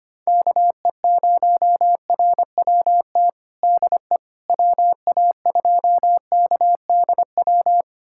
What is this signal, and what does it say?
Morse code, heard as a keyed tone, about 25 wpm, 700 Hz. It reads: KE0RWT DE WA2KDW
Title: USA Callsign DE Callsign - 6 Characters - 25wpm